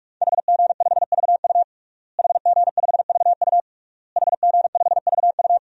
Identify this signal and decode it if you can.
Morse code, heard as a keyed tone, about 46 wpm, 700 Hz. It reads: HG5VU HG5VU HG5VU
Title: DXCC-100 Call Signs - 5 Characters - Repeated 3x - 45wpm